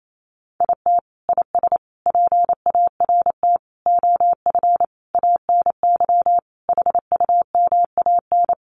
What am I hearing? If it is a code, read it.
Morse code, heard as a keyed tone, about 28 wpm, 700 Hz. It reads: IT IS PART OF ANY HUMAN
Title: Sentences from Top 1100 Words 28wpm